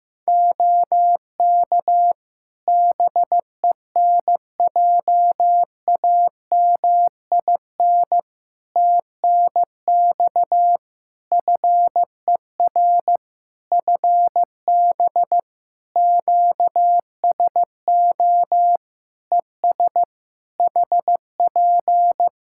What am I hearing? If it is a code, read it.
Morse code, heard as a keyed tone, about 15 wpm, 700 Hz. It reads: OK BENJAMIN TNX FER FB QSO ES HP
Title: Practice QSO - Exchange 4 - Ending - 15wpm